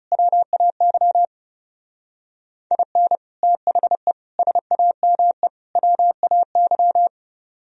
Morse code, heard as a keyed tone, about 39 wpm, 700 Hz. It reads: WAY IN THE SAME WAY